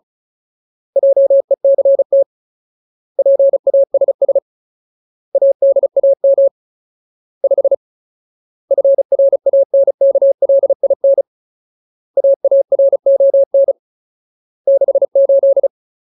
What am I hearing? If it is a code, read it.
Morse code, heard as a keyed tone, about 34 wpm, 550 Hz. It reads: JECT PASS ADAM 5 FRANKLIN AARON 68
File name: Ultimate Word List / 35WPM Ultimate